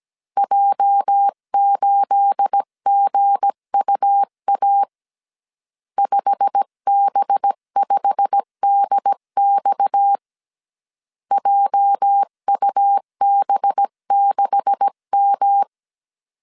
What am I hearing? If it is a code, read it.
Morse code, heard as a keyed tone, about 17 wpm, 800 Hz. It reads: J8GUA 5B5DX JUB6M
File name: CW / DET / 1 9 1500C S B 16wpm 2856